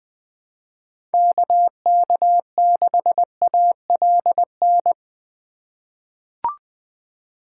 Morse code, heard as a keyed tone, about 20 wpm, 700 Hz. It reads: KK6ALN
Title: USA Call Signs - 6 Characters - 3x Character Spacing - 20wpm